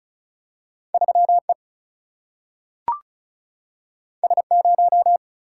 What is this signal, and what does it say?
Morse code, heard as a keyed tone, about 36 wpm, 700 Hz. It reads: 3 E S 0
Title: Single-Letter Single-Number 35wpm